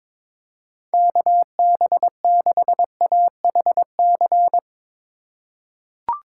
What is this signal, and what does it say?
Morse code, heard as a keyed tone, about 22 wpm, 700 Hz. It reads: KB6AHC
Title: USA Call Signs - 6 Characters - 4x Character Spacing - 22wpm